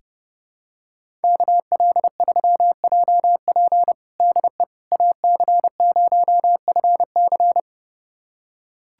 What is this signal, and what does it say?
Morse code, heard as a keyed tone, about 30 wpm, 700 Hz. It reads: KL3JP DE AC0FC
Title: USA Callsign DE Callsign - 5 Characters - 30wpm